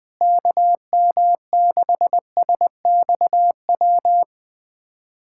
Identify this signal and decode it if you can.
Morse code, heard as a keyed tone, about 21 wpm, 700 Hz. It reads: KM6SXW
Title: USA Callsign DE Callsign 20wpm